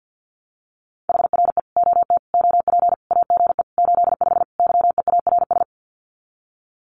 Morse code, heard as a keyed tone, about 50 wpm, 700 Hz. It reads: HR E OT OP AGE 85 YEARS
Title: Common QSO Phrases 50wpm